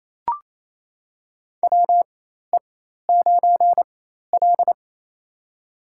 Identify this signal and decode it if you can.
Morse code, heard as a keyed tone, about 28 wpm, spaced as for 13 wpm, 700 Hz. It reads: WE9L
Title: USA Call Signs - 4x Character Spacing - 28wpm